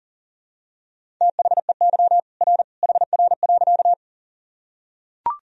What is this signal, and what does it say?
Morse code, heard as a keyed tone, about 37 wpm, 700 Hz. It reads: THEY R HR.
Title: Sentences from Top 200 Words and Abbreviations 40wpm